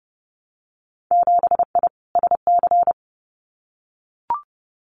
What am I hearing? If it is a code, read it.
Morse code, heard as a keyed tone, about 30 wpm, 700 Hz. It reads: 7I SC
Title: Winter Field Day Exchanges Encore 30wpm